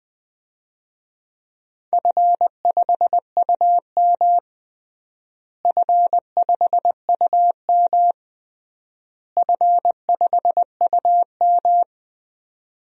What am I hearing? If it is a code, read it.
Morse code, heard as a keyed tone, about 19 wpm, 700 Hz. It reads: F5UM F5UM F5UM
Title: DXCC-100 Call Signs - 4 Characters - Repeated 3x - 20wpm